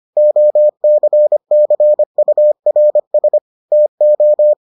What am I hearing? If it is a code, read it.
Morse code, heard as a keyed tone, about 25 wpm, 600 Hz. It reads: OCCURS TO